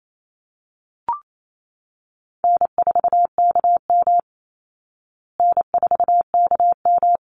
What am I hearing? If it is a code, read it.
Morse code, heard as a keyed tone, about 28 wpm, 700 Hz. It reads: N4KM N4KM
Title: Easy Call Signs Repeated 2x - 28wpm